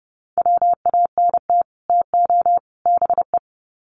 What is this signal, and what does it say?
Morse code, heard as a keyed tone, about 30 wpm, 700 Hz. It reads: WANT TO BE
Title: Sets of 3 Words 30wpm